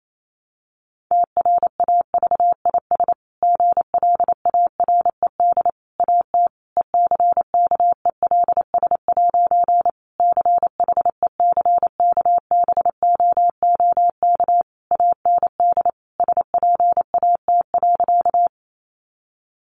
Morse code, heard as a keyed tone, about 28 wpm, 700 Hz. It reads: TRAVIS GLARED AT ECKELS' CHECKBOOK AND SPAT.
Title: A Sound Of Thunder 28wpm